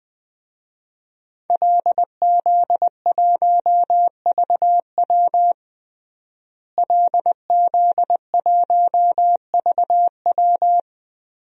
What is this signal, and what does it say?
Morse code, heard as a keyed tone, about 21 wpm, 700 Hz. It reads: LZ1VW LZ1VW
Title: DXCC-100 Call Signs - 5 Characters - Repeated 3x - 20wpm